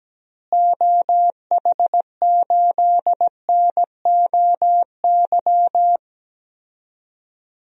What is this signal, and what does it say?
Morse code, heard as a keyed tone, about 17 wpm, 700 Hz. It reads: OH8NOY
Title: DXCC-100 Call Signs - 6 Characters - Repeated 3x - 17wpm